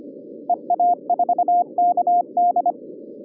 Morse code, heard as a keyed tone, about 25 wpm, 710 Hz. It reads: EA4KD